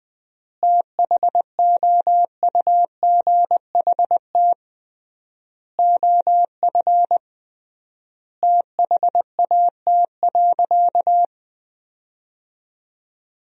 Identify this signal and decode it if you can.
Morse code, heard as a keyed tone, about 21 wpm, 700 Hz. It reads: THOUGHT OF THAT.
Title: 3x Word-Spacing - Sentences from Top 300 Words - 20wpm